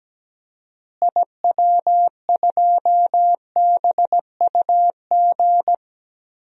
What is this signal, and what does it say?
Morse code, heard as a keyed tone, about 17 wpm, 700 Hz. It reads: IW2BUG
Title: DXCC-100 Call Signs - 6 Characters - Repeated 2x - 17wpm